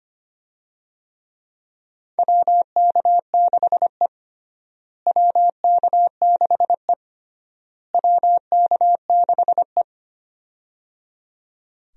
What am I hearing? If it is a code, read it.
Morse code, heard as a keyed tone, about 25 wpm, 700 Hz. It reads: WK6E WK6E WK6E
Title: USA Call Signs - 4 Characters - Repeated 3x - 25wpm